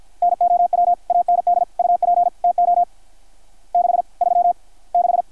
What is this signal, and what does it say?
Morse code, heard as a keyed tone, about 53 wpm, 720 Hz. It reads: NOW AND UP TO 6 3 6
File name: archived / CRSnet